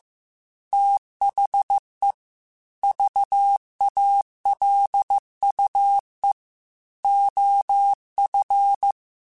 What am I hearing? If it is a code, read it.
Morse code, heard as a keyed tone, about 15 wpm, 800 Hz. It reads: THE VALUE OF
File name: Technical Subjects / Technical Subjects-mp3s / XL @13